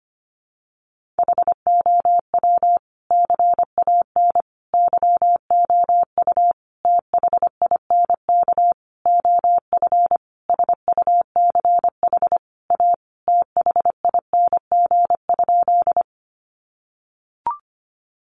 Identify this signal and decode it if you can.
Morse code, heard as a keyed tone, about 25 wpm, 700 Hz. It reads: HOW CAN YOU THINK OF SUCH A THING?
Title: Sentences from Top 200 Words 25wpm